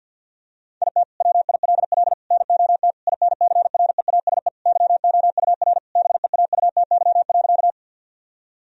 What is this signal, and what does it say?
Morse code, heard as a keyed tone, about 50 wpm, 700 Hz. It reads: IT WILL NOT INCREASE YOUR BEAUTY.
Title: Sentences from Top 1000 Words - Review Part 4 - 50wpm